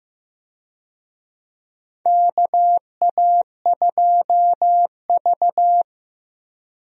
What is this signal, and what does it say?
Morse code, heard as a keyed tone, about 15 wpm, 700 Hz. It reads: KA2V
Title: USA Call Signs - 4 Characters - Repeated 3x - 15wpm